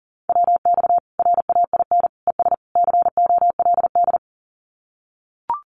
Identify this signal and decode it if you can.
Morse code, heard as a keyed tone, about 40 wpm, 700 Hz. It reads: WX RAIN ES COLD
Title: Common QSO Phrases 40wpm